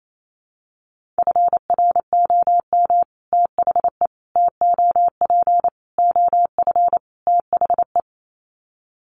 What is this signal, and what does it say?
Morse code, heard as a keyed tone, about 28 wpm, 700 Hz. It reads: FROM THE TOP OF THE
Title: Sets of 5 Words Encore 28wpm